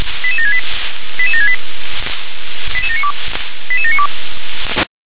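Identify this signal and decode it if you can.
Five-tone selective call (ZVEI), heard as quick runs of five tones, tones of about 70 ms. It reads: 88768 88768 88792 88792
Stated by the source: ZVEI five-tone call 88657 received by SDR.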